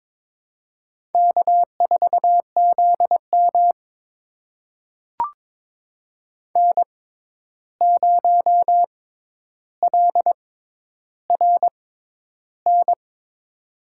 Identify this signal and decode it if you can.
Morse code, heard as a keyed tone, about 22 wpm, spaced as for 11 wpm, 700 Hz. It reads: K4ZM N0LRN
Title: USA Call Signs - 6x Character Spacing - 22wpm